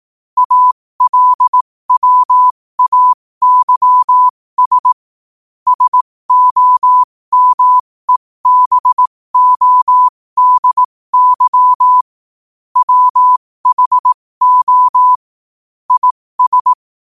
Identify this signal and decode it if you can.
Morse code, heard as a keyed tone, about 18 wpm, spaced as for 15 wpm, 1000 Hz. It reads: ALWAYS SOMEBODY WHO IS
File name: qotdfiles / 15wpm